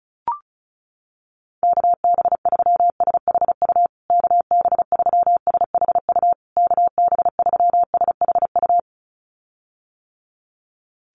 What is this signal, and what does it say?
Morse code, heard as a keyed tone, about 34 wpm, 700 Hz. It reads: KB3SHU KB3SHU KB3SHU
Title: USA Call Signs Repeated 3x 35wpm